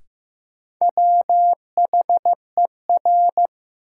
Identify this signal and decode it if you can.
Morse code, heard as a keyed tone, about 15 wpm, 700 Hz. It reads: WHER
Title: Sentences from Top 1500 Words - New Word Drill - 15wpm